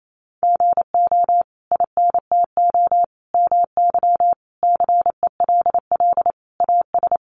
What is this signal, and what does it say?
Morse code, heard as a keyed tone, about 28 wpm, 700 Hz. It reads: GO INTO MY CELL AS
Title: Sentences from Top 1100 Words 28wpm